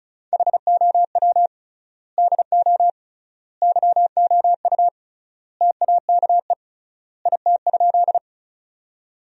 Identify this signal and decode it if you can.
Morse code, heard as a keyed tone, about 34 wpm, 700 Hz. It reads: HOW DO YOU TAKE IT?
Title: 3x Word-Spacing - Sentences from Top 200 Words - 35wpm